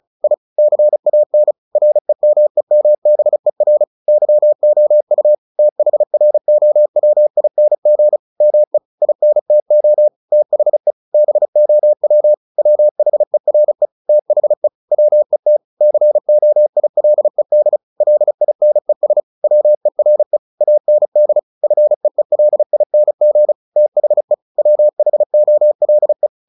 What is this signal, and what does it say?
Morse code, heard as a keyed tone, about 34 wpm, 600 Hz. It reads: I CAN REMEMBER YOU THROWING ME INTO THE BOW WHERE THE WET COILED LINES WERE AND FEELING THE WHOLE
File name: the old man and the sea / the old man and the sea 1 35wpm